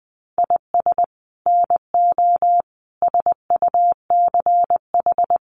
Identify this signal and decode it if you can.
Morse code, heard as a keyed tone, about 19 wpm, 700 Hz. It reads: IS NO SUCH
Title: Sets of 5 Words Encore 20wpm